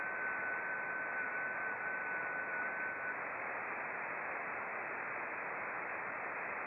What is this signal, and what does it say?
FT8.
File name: contests / sept-vhf-k5tr- -0300